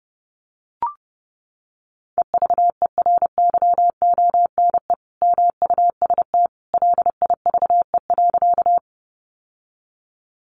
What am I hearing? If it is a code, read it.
Morse code, heard as a keyed tone, about 30 wpm, 700 Hz. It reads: EVERYONE MUST LIVE.
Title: Sentences from Top 800 Words 30wpm